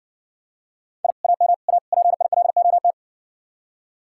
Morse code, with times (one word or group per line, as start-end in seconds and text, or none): 1.0-1.1 I
1.2-1.5 AM
1.7-1.8 A
1.9-2.9 PILOT